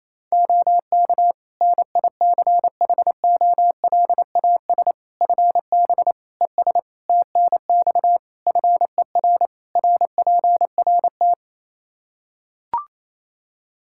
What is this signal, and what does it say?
Morse code, heard as a keyed tone, about 28 wpm, 700 Hz. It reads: OK NICHOLAS FB ES TNX FER RPRT
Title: Practice QSO - Exchange 2 - Info - 28wpm